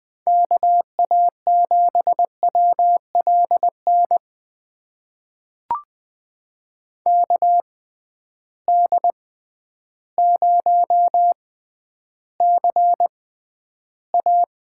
Morse code, heard as a keyed tone, about 20 wpm, spaced as for 11 wpm, 700 Hz. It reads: KA7WLN KD0CA
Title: USA Call Signs - 6x Character Spacing - 20wpm